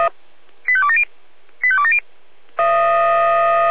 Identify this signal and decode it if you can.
Five-tone selective call (ZVEI), heard as quick runs of five tones, tones of about 80 ms. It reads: 85270 85270